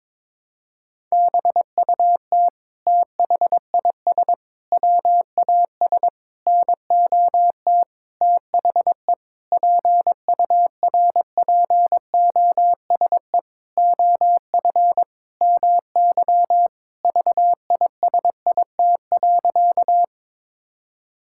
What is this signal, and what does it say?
Morse code, heard as a keyed tone, about 22 wpm, 700 Hz. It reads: BUT THIS WAS NOT THE PURPOSE OF MY VISIT.
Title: Sentences from Top 1000 Words - Review Part 6 - 22wpm